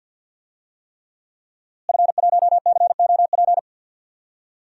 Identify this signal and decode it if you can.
Morse code, heard as a keyed tone, about 48 wpm, 700 Hz. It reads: F1COP
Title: DXCC-100 Call Signs - 5 Characters - Repeated 3x - 50wpm